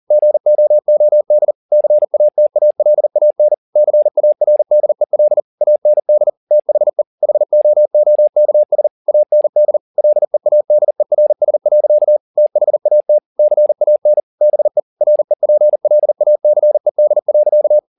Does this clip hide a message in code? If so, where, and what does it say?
Yes: Morse code from 0.1 to 17.8 s, heard as a keyed tone, about 40 wpm, 600 Hz. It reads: GOOD CATALAN CARDEL AND THE HOOKS AND LEADERS. THAT CAN BE REPLACED.